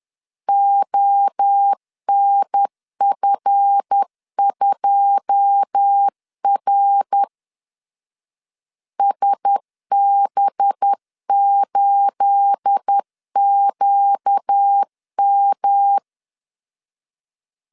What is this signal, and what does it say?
Morse code, heard as a keyed tone, about 10 wpm, 800 Hz. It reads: ONF2R SB8QM
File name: CW / DET / 1 9 1500C S B 10wpm 4439